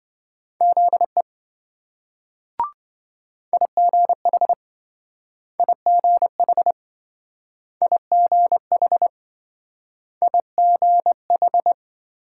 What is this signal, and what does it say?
Morse code, heard as a keyed tone, about 27 wpm, 700 Hz. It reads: ZE IGH IGH IGH IGH